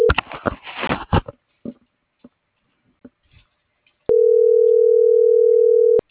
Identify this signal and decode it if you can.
Ringback tone (North America), 440 + 480 Hz.